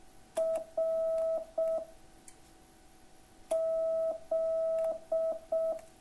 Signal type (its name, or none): Morse code